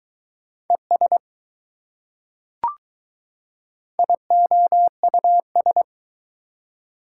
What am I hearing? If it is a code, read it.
Morse code, heard as a keyed tone, about 20 wpm, 700 Hz. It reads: ES IOUS